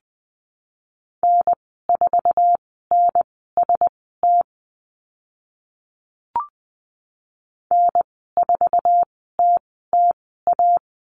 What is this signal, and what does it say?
Morse code, heard as a keyed tone, about 20 wpm, spaced as for 14 wpm, 700 Hz. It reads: N4NST N4TTA
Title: 062 Lesson 11g - Letter h - Call Signs 20wpm